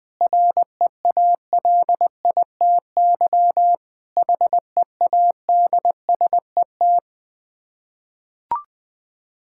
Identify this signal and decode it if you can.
Morse code, heard as a keyed tone, about 20 wpm, 690 Hz. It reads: REALITY HEADSET
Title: Difficult Words with Hints - 20wpm